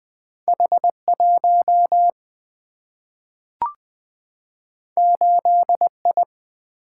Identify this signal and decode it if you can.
Morse code, heard as a keyed tone, about 20 wpm, 700 Hz. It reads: H1 8I
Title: Single-Letter Single-Number 20wpm